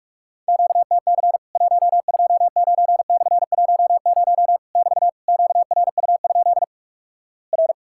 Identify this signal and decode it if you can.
Morse code, heard as a keyed tone, about 46 wpm, 700 Hz. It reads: QTC 129/10 <BT> QRU? R
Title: Worked All Europe DX Contest - QTC Exchange - 45wpm